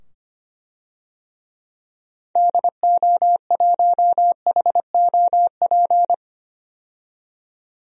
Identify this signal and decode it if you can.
Morse code, heard as a keyed tone, about 25 wpm, 700 Hz. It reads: DO1HOP
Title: DXCC-100 Call Signs - 6 Characters - 25wpm